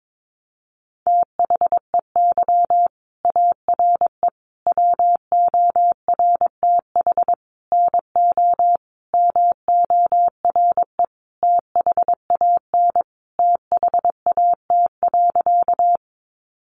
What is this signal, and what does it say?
Morse code, heard as a keyed tone, about 22 wpm, 700 Hz. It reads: THEY ARE WORTH NO MORE THAN THAT.
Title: Sentences from Top 800 Words 22wpm